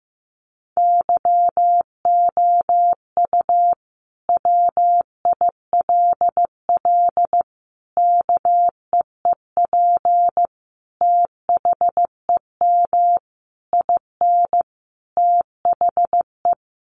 Morse code, heard as a keyed tone, about 15 wpm, 700 Hz. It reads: YOU WILL KEEP THEM IN THE
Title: Sentences from Top 1300 Words 15wpm